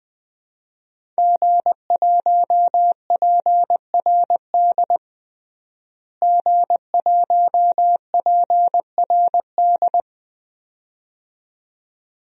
Morse code, heard as a keyed tone, about 21 wpm, 700 Hz. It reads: G1PRD G1PRD
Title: DXCC-100 Call Signs - 5 Characters - Repeated 2x - 20wpm